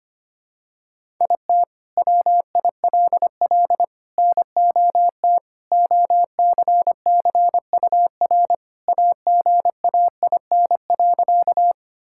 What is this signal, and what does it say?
Morse code, heard as a keyed tone, about 25 wpm, 700 Hz. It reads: IT WILL NOT OCCUR AGAIN.
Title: Sentences from Top 1000 Words - Review Part 6 - 25wpm